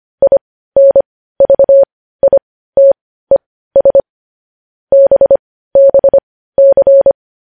Morse code, heard as a keyed tone, about 25 wpm, spaced as for 15 wpm, 550 Hz. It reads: INVITES BBC
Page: news headlines in morse code at 15 wpm